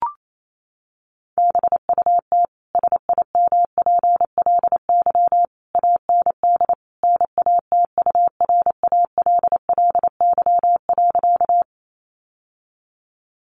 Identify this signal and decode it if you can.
Morse code, heard as a keyed tone, about 28 wpm, 700 Hz. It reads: BUT SIMPLY AND NATURALLY.
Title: Sentences from Top 1300 Words 28wpm